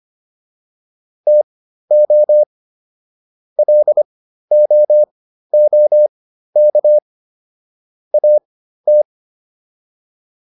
Morse code, heard as a keyed tone, about 25 wpm, spaced as for 13 wpm, 600 Hz. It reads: TO LOOK AT